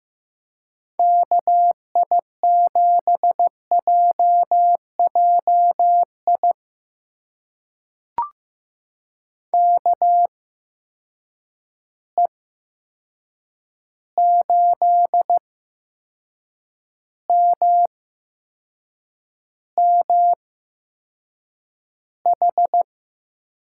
Morse code, heard as a keyed tone, about 15 wpm, spaced as for 6 wpm, 700 Hz. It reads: KI7JJI KE8MMH
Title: USA Call Signs - 6 Characters - 8x Character Spacing - 15wpm